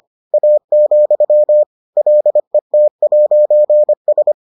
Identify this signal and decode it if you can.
Morse code, heard as a keyed tone, about 25 wpm, 600 Hz. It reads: A, LET'S